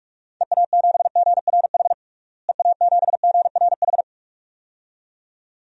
Morse code, heard as a keyed tone, about 41 wpm, 700 Hz. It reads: EA7GRH EA7GRH
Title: DXCC-100 Call Signs - 6 Characters - Repeated 2x - 45wpm